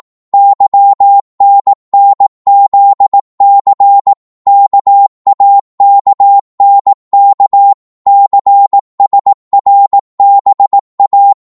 Morse code, heard as a keyed tone, about 18 wpm, 800 Hz. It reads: YNNZC KAKNK CSRBA
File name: Morse / ltr-18wpm